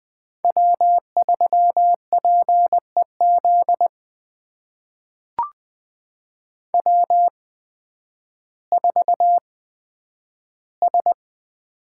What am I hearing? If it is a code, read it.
Morse code, heard as a keyed tone, about 20 wpm, spaced as for 11 wpm, 700 Hz. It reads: W3PEZ W4S
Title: USA Call Signs - 5 Characters - 8x Character Spacing - 20wpm